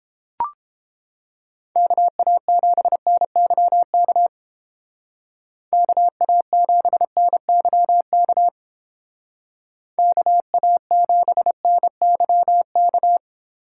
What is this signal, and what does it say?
Morse code, heard as a keyed tone, about 30 wpm, 700 Hz. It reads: KA7NYK KA7NYK KA7NYK